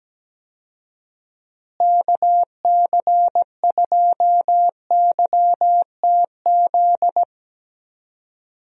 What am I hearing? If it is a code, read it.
Morse code, heard as a keyed tone, about 17 wpm, 700 Hz. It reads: KC2YTZ
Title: USA Call Signs - 6 Characters - Repeated 2x - 17wpm